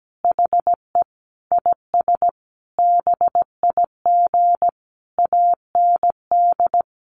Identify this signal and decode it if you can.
Morse code, heard as a keyed tone, about 17 wpm, 700 Hz. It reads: HE IS BIG AND
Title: Sentences from Top 1300 Words 17wpm